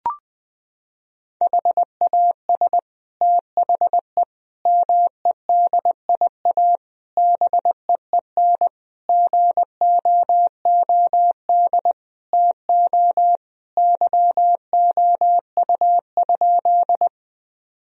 Morse code, heard as a keyed tone, about 20 wpm, 700 Hz. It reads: HAS THE MEDIA BEEN GOOD TO YOU?
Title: Sentences from Top 1000 Words - Review Part 4 - 20wpm